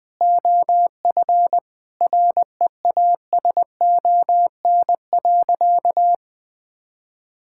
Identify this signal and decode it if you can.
Morse code, heard as a keyed tone, about 20 wpm, 700 Hz. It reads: OF REASON.
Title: Sentences from Top 1000 Words - Review Part 6 - 20wpm